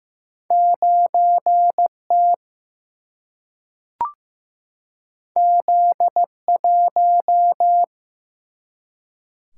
Morse code, heard as a keyed tone, about 15 wpm, 700 Hz. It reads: 9 T Z 1
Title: Single-Letter Single-Number 15wpm